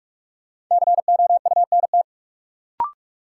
Morse code, heard as a keyed tone, about 46 wpm, 700 Hz. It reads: COUNT
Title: Sentences from Top 1500 Words - New Word Drill - 45wpm